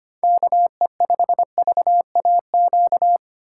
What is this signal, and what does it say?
Morse code, heard as a keyed tone, about 25 wpm, 700 Hz. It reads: KE5VAQ
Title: USA Call Signs - 6 Characters - 8x Character Spacing - 25wpm